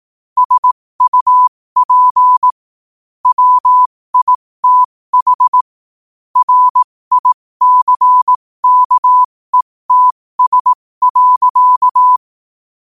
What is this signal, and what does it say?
Morse code, heard as a keyed tone, about 18 wpm, spaced as for 15 wpm, 1000 Hz. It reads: SUP WITH RICKETS.
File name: qotdfiles / 15wpm